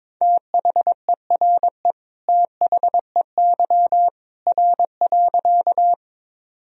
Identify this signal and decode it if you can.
Morse code, heard as a keyed tone, about 22 wpm, 700 Hz. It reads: THERE THEY R.
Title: Sentences from Top 100 Words and Abbreviations 22wpm